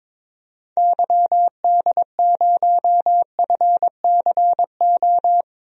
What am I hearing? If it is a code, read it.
Morse code, heard as a keyed tone, about 22 wpm, 700 Hz. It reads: YD0FCO
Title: DXCC-100 Call Signs - 6 Characters - Repeated 2x - 22wpm